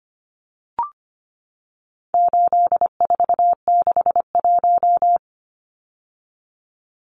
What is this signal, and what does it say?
Morse code, heard as a keyed tone, about 25 wpm, 700 Hz. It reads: 8461
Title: Numbers - 4-Digit 25wpm